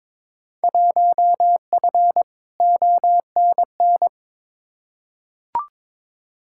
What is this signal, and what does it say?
Morse code, heard as a keyed tone, about 22 wpm, 700 Hz. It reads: 1F ONN
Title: Field Day Exchanges Encore - 22wpm